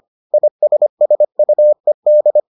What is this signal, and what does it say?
Morse code, heard as a keyed tone, about 25 wpm, 600 Hz. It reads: ISSUED